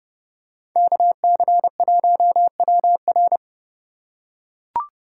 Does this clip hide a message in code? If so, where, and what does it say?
Yes: Morse code from 0.8 to 3.4 s, heard as a keyed tone, about 30 wpm, 700 Hz. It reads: KC1WR